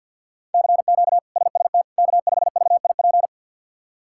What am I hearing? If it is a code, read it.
Morse code, heard as a keyed tone, about 46 wpm, 700 Hz. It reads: CQ SST K5VIP
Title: K1USN SST QSO Practice - 50wpm